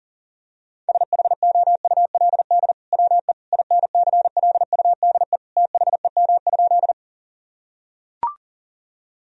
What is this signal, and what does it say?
Morse code, heard as a keyed tone, about 40 wpm, 700 Hz. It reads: SHOULD WE INCLUDE THEM?
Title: Sentences from Top 1000 Words - Review Part 3 - 40wpm